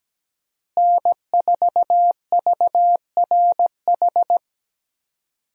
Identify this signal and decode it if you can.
Morse code, heard as a keyed tone, about 17 wpm, 700 Hz. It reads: N4VRH
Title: USA Call Signs - 5 Characters - Repeated 2x - 17wpm